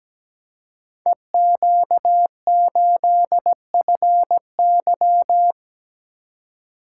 Morse code, heard as a keyed tone, about 17 wpm, 700 Hz. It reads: EQ8FY
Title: Code Groups - 5 Letter:Numbers - 17wpm